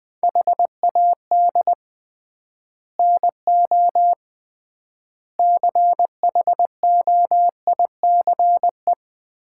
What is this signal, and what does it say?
Morse code, heard as a keyed tone, about 20 wpm, 690 Hz. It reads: HAD NO CHOICE
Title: 3x Word-Spacing - Sets of 5 Words - 20wpm